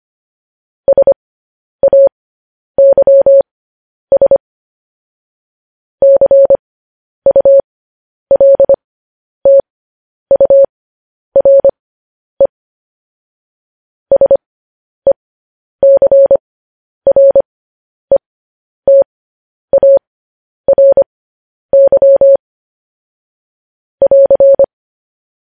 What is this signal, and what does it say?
Morse code, heard as a keyed tone, about 25 wpm, spaced as for 10 wpm, 550 Hz. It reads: SAYS CULTURE SECRETARY <AR>
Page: News Headlines in Morse Code at 10 WPM